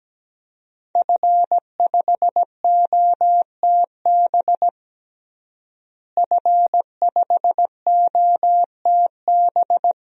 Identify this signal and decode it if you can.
Morse code, heard as a keyed tone, about 17 wpm, 700 Hz. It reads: F5OTB F5OTB
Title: DXCC-100 Call Signs - 5 Characters - Repeated 3x - 17wpm